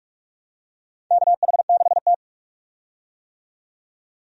Morse code, heard as a keyed tone, about 46 wpm, 700 Hz. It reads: KH6T
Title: USA Call Signs - 4 Characters - 45wpm